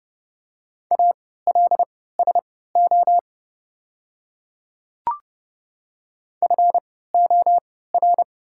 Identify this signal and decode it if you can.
Morse code, heard as a keyed tone, about 30 wpm, spaced as for 17 wpm, 700 Hz. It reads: ALSO FOR
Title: 3x Character Spacing - Top 300 Words - 30wpm